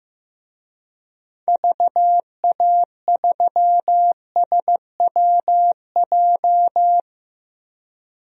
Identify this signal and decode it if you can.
Morse code, heard as a keyed tone, about 15 wpm, 700 Hz. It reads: VA3SWJ
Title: DXCC-100 Call Signs - 6 Characters - 15wpm